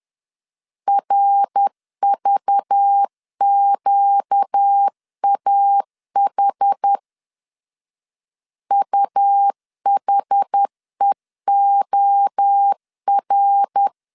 Morse code, heard as a keyed tone, about 10 wpm, 800 Hz. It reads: RVQAH UHEOR
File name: LEZIONI / 6 PKX0 / 6 PKX0 10wpm 2707